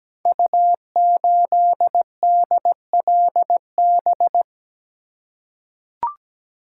Morse code, heard as a keyed tone, about 17 wpm, 700 Hz. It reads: U8DLB
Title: Code Groups - 5 Letter:Numbers - 17wpm